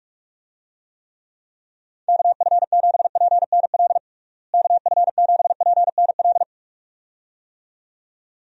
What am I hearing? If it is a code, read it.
Morse code, heard as a keyed tone, about 46 wpm, 700 Hz. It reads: KF7PNL KF7PNL
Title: USA Call Signs - 6 Characters - Repeated 2x - 45wpm